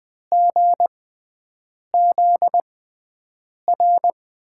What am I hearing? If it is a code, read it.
Morse code, heard as a keyed tone, about 21 wpm, spaced as for 7 wpm, 700 Hz. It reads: GZR
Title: USA Call Signs - 5 Characters - 6x Character Spacing - 20wpm